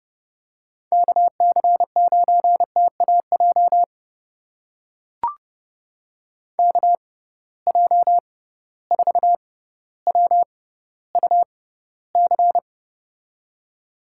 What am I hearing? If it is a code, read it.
Morse code, heard as a keyed tone, about 32 wpm, spaced as for 16 wpm, 700 Hz. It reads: KC9TAJ KJ4WUC
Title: USA Call Signs - 6 Characters - 6x Character Spacing - 30wpm